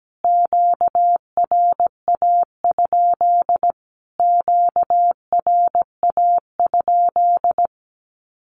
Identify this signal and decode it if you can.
Morse code, heard as a keyed tone, about 17 wpm, 700 Hz. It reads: QRA? QRA?
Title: Q-Codes Repeated 3x 17wpm